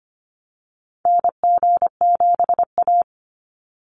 Morse code, heard as a keyed tone, about 25 wpm, 700 Hz. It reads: NG7A
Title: NA SOTA Activator Callsigns 25wpm